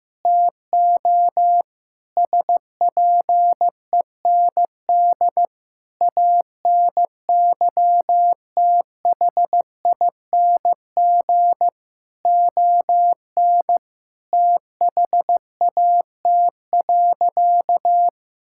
Morse code, heard as a keyed tone, about 15 wpm, 700 Hz. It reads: TO SPEND ANYTHING ON THAT.
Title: Sentences from Top 1000 Words - Review Part 5 - 15wpm